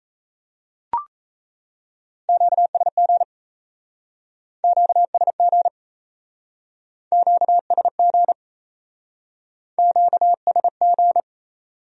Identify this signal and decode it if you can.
Morse code, heard as a keyed tone, about 39 wpm, 700 Hz. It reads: QSG QSG QSG QSG